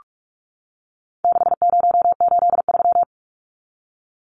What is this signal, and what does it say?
Morse code, heard as a keyed tone, about 44 wpm, 700 Hz. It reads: 6083
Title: Numbers - 4-Digit 45wpm